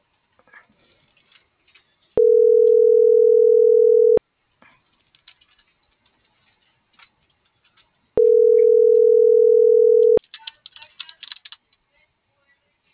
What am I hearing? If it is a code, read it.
Ringback tone (North America), 440 + 480 Hz.